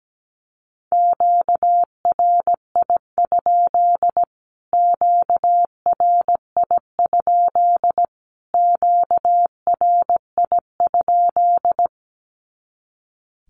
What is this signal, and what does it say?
Morse code, heard as a keyed tone, about 17 wpm, 700 Hz. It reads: QRI? QRI? QRI?
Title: Q-Codes Repeated 3x 17wpm